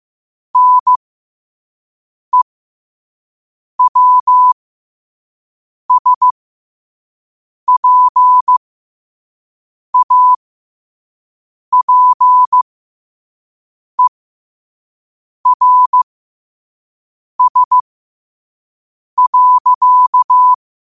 Morse code, heard as a keyed tone, about 15 wpm, spaced as for 5 wpm, 1000 Hz. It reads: NEWSPAPERS.
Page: New quotes every day in morse code at 5 Words per minute.